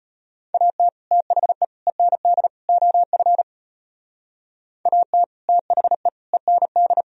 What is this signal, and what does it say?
Morse code, heard as a keyed tone, about 31 wpm, 700 Hz. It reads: AT THE END OF AT THE END
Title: Speed-Racing - Sets of 4 Words 25wpm